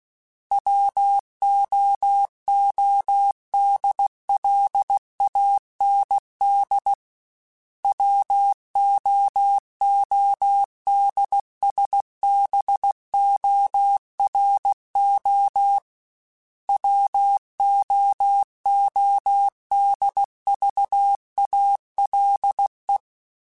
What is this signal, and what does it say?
Morse code, heard as a keyed tone, about 16 wpm, 800 Hz. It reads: WOODLAND WOODSBORO WOODVALE
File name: fromSMs / FromSMs mp3s / Cities@15